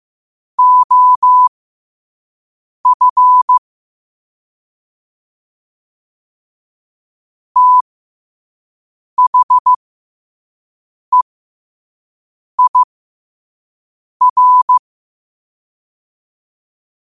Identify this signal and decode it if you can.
Morse code, heard as a keyed tone, about 15 wpm, spaced as for 5 wpm, 1000 Hz. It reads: OF THEIR
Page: New quotes every day in morse code at 5 Words per minute.